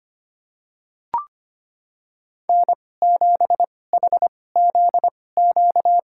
Morse code, heard as a keyed tone, about 25 wpm, spaced as for 18 wpm, 700 Hz. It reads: N7HZQ
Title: USA Call Signs - 5 Characters - 2x Character Spacing - 25wpm